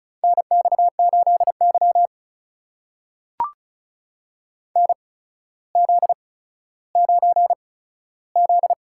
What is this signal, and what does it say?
Morse code, heard as a keyed tone, about 36 wpm, spaced as for 15 wpm, 700 Hz. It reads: NX8Y NZ9Z
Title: USA Call Signs - 4 Characters - 8x Character Spacing - 35wpm